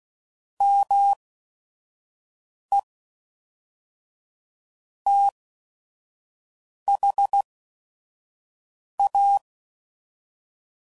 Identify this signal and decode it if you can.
Morse code, heard as a keyed tone, about 16 wpm, spaced as for 5 wpm, 800 Hz. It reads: ME THA
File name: CW Code Groups @5WPM